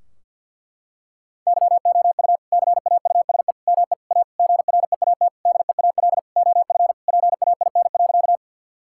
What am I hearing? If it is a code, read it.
Morse code, heard as a keyed tone, about 51 wpm, 700 Hz. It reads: YOU CAUSE ME A GREAT DEAL OF PAIN.
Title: Sentences from Top 1600 Words 50wpm